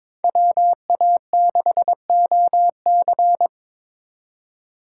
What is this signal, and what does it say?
Morse code, heard as a keyed tone, about 25 wpm, 700 Hz. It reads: WA6OC
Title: Speed Racing - USA 2x2 Call Signs 22wpm